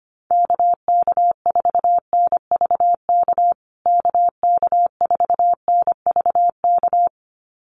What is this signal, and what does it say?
Morse code, heard as a keyed tone, about 25 wpm, 700 Hz. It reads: KK4NVK KK4NVK
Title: USA Call Signs Repeated 2x 25wpm